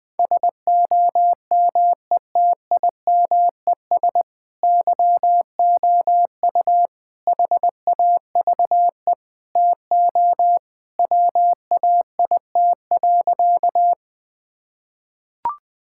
Morse code, heard as a keyed tone, about 21 wpm, 700 Hz. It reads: SOMETIMES YOU HAVE TO WAIT.
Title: Sentences with 3-Word Phrases - 20wpm